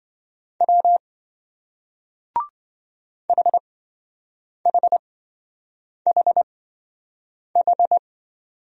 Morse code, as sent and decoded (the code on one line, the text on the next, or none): .-- / .... .... .... ....
W HHHH